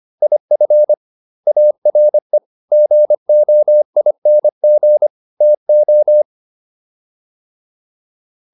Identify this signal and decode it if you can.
Morse code, heard as a keyed tone, about 25 wpm, 600 Hz. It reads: IF ARE GOING TO